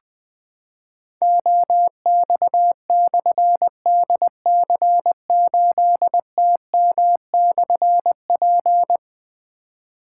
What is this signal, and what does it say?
Morse code, heard as a keyed tone, about 20 wpm, 700 Hz. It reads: OX/DC8TM/P
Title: Difficult Call Signs 20wpm